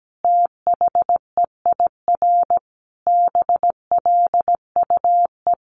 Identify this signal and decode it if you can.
Morse code, heard as a keyed tone, about 17 wpm, 700 Hz. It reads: THEIR BLUE
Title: A Sound Of Thunder 17wpm